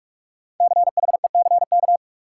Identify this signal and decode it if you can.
Morse code, heard as a keyed tone, about 46 wpm, 700 Hz. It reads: CHECK
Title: Sentences from Top 1400 Words - New Word Drill Alt - 45wpm